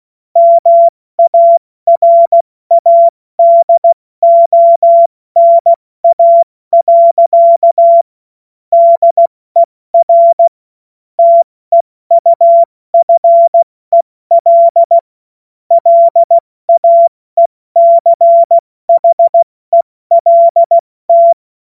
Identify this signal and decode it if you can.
Morse code, heard as a keyed tone, about 16 wpm, 680 Hz. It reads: MARADONA. DER TEUFEL LAECHELT
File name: Morsekurs / Witze / Witz 02. Februar 2021, 70 F80